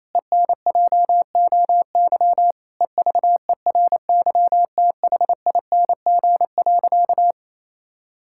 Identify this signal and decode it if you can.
Morse code, heard as a keyed tone, about 28 wpm, 700 Hz. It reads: ENJOY EVERYTHING.
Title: Sentences from Top 1000 Words - Review Part 5 - 28wpm